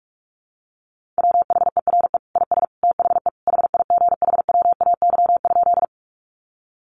Morse code, heard as a keyed tone, about 44 wpm, 700 Hz. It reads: WHERE IS THE HIGHWAY?
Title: Sentences from Top 1400 Words 45wpm